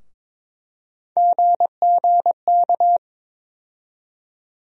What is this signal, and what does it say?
Morse code, heard as a keyed tone, about 22 wpm, 700 Hz. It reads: GGK
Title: USA Call Sign Postfixes - 3 Letters 22wpm